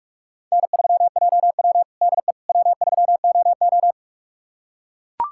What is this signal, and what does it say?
Morse code, heard as a keyed tone, about 44 wpm, 690 Hz. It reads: N3JW DE W3OO
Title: USA Callsign DE Callsign - 4 Characters - 45wpm